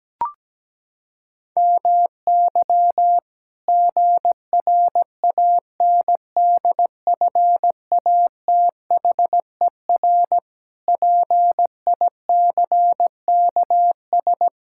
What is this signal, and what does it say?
Morse code, heard as a keyed tone, about 17 wpm, 700 Hz. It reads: MY GRANDFATHER PICKS
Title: Pangram Challenge v2 - 17wpm